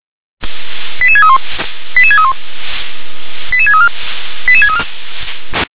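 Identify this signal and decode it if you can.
Five-tone selective call (ZVEI), heard as quick runs of five tones, tones of about 70 ms. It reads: 88631 88631 88634 88634
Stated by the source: ZVEI five-tone selective call received by SDR.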